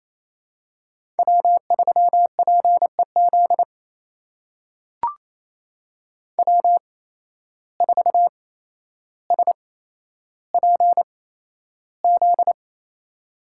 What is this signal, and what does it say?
Morse code, heard as a keyed tone, about 28 wpm, spaced as for 12 wpm, 700 Hz. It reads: W3PEZ W4SPZ
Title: USA Call Signs - 5 Characters - 8x Character Spacing - 28wpm